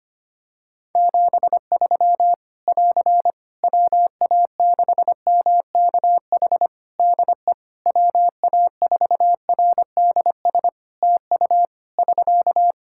Morse code, heard as a keyed tone, about 25 wpm, 700 Hz. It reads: 73 <AR> WA6MKH DE WA4RDS TU <SK>
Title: Practice QSO - Exchange 4 - Ending - 25wpm